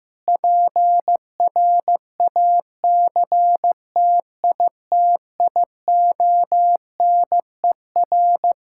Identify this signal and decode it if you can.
Morse code, heard as a keyed tone, about 15 wpm, 700 Hz. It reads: PRACTITIONER
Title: Practice QSO - Exchange 3 - More Info - 15wpm